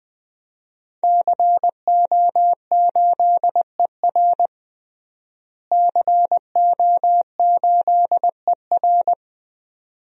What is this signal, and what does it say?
Morse code, heard as a keyed tone, about 20 wpm, 700 Hz. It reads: CO8ER CO8ER
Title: DXCC-100 Call Signs - 5 Characters - Repeated 2x - 20wpm